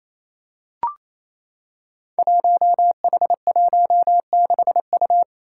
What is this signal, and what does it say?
Morse code, heard as a keyed tone, about 27 wpm, 700 Hz. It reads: 1H16U
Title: Five Challenging Letter-Number Combinations 28wpm